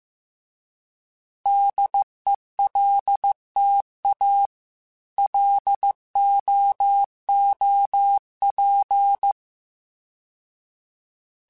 Morse code, heard as a keyed tone, about 15 wpm, 800 Hz. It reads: DELTA LOOP